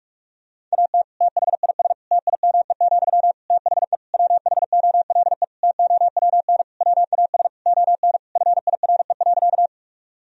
Morse code, heard as a keyed tone, about 46 wpm, 700 Hz. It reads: AT THIS TIME, THE WHOLE TOWN WAS ON FIRE.
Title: Sentences from Top 1000 Words - Review Part 2 - 45wpm